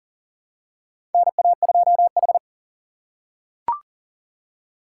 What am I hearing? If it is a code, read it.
Morse code, heard as a keyed tone, about 38 wpm, 700 Hz. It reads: NA2H
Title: USA Call Signs - 4 Characters - 6x Character Spacing - 40wpm